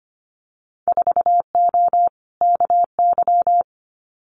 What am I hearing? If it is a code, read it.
Morse code, heard as a keyed tone, about 25 wpm, 700 Hz. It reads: 4O KY
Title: Winter Field Day Exchanges Encore 25wpm